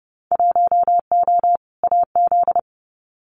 Morse code, heard as a keyed tone, about 30 wpm, 700 Hz. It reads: 1O AZ
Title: Winter Field Day Exchanges 30wpm